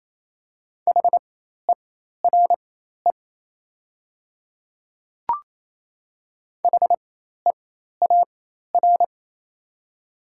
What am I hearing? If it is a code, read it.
Morse code, heard as a keyed tone, about 28 wpm, spaced as for 13 wpm, 700 Hz. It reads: HERE HEAR
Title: 4x Character Spacing - Top 300 Words - 28wpm